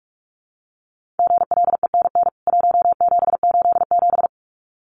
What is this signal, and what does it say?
Morse code, heard as a keyed tone, about 44 wpm, 700 Hz. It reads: GLENN 1787
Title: CWT Exchange 45wpm